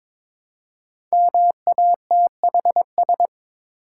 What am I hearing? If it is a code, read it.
Morse code, heard as a keyed tone, about 22 wpm, 690 Hz. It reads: MATHS
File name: 5 Letter Words - First 3 Letters Given - 22wpm